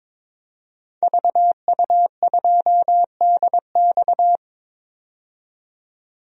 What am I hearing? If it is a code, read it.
Morse code, heard as a keyed tone, about 22 wpm, 700 Hz. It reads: VU2DX
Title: DXCC-100 Call Signs - 5 Characters - 22wpm